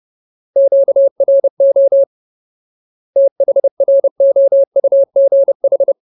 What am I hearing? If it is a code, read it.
Morse code, heard as a keyed tone, about 30 wpm, 550 Hz. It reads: QRO THROUGH
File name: Ultimate Word List / 30WPM Ultimate